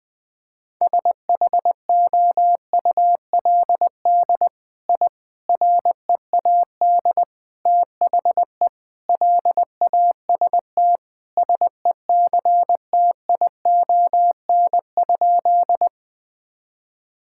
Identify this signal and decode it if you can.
Morse code, heard as a keyed tone, about 21 wpm, 700 Hz. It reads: SHOULD I READ THE LAST SECTION?
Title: Sentences from Top 1000 Words - Review Part 5 - 20wpm